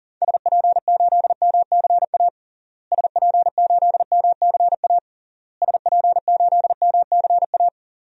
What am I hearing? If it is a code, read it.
Morse code, heard as a keyed tone, about 40 wpm, 700 Hz. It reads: SP8MCA SP8MCA SP8MCA
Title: DXCC-100 Call Signs - 6 Characters - Repeated 3x - 40wpm